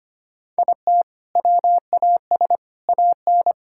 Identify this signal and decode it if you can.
Morse code, heard as a keyed tone, about 25 wpm, 700 Hz. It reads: IT WAS AN
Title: Sentences from Top 1000 Words - Review Part 5 - 25wpm